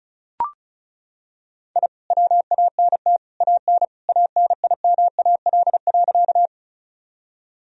Morse code, heard as a keyed tone, about 36 wpm, 700 Hz. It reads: I WANT AN ANIMAL.
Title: Sentences from Top 1000 Words - Review Part 1 - 35wpm